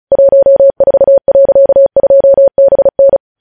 Morse code, heard as a keyed tone, about 36 wpm, 550 Hz. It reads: 14.2BN
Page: news headlines in morse code at 35 wpm